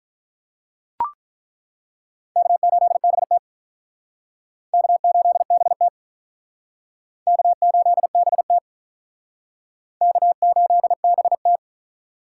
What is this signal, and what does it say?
Morse code, heard as a keyed tone, about 48 wpm, 700 Hz. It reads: K8BT K8BT K8BT K8BT